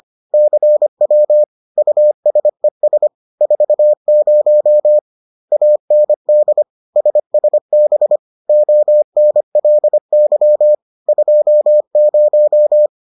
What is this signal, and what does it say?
Morse code, heard as a keyed tone, about 25 wpm, 600 Hz. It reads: CW USES 40 AND SSB ONLY 20